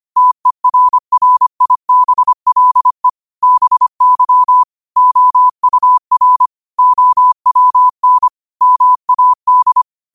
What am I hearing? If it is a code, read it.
Morse code, heard as a keyed tone, about 25 wpm, 1000 Hz. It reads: TERRIBLE BY OUR OWN MAD
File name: 25wpm